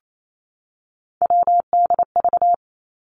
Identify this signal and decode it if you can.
Morse code, heard as a keyed tone, about 28 wpm, 700 Hz. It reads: WDV
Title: Three Challenging Letter-Number Combinations 28wpm